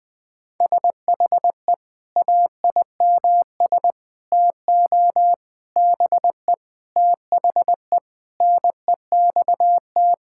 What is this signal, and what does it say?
Morse code, heard as a keyed tone, about 20 wpm, 700 Hz. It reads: SHE AIMS TO BE THE NEXT
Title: Sentences with 9-Word Phrases - 20wpm